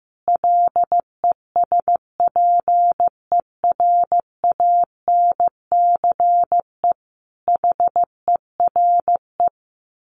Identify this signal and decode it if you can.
Morse code, heard as a keyed tone, about 15 wpm, 700 Hz. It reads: LESPERANCE HERE
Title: A Sound Of Thunder 15wpm